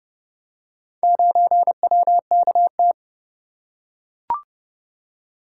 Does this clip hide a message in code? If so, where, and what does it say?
Yes: Morse code from 1.0 to 2.9 s, heard as a keyed tone, about 30 wpm, 700 Hz. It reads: 9WKT